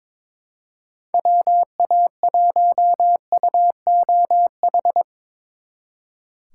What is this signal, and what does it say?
Morse code, heard as a keyed tone, about 25 wpm, 700 Hz. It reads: WA1UOH
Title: Speed Racing - USA 2x3 Call Signs 22wpm